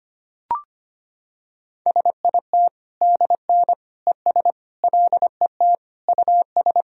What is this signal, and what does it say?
Morse code, heard as a keyed tone, about 25 wpm, 700 Hz. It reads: SIT DN ES LET US
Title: Sentences from Top 500 Words and Abbreviations 25wpm